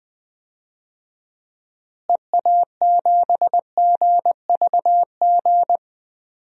Morse code, heard as a keyed tone, about 20 wpm, 700 Hz. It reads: EA7GVG
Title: DXCC-100 Call Signs - 6 Characters - Repeated 3x - 20wpm